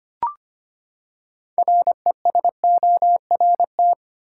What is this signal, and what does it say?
Morse code, heard as a keyed tone, about 25 wpm, 700 Hz. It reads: RESORT
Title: Six Letter Words - 25wpm